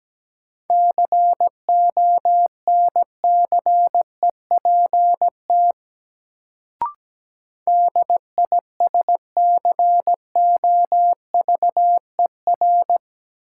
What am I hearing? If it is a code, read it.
Morse code, heard as a keyed tone, about 17 wpm, 700 Hz. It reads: CONCEPT DISCOVER
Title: Sentences from Top 1100 Words - New Word Drill Alt - 17wpm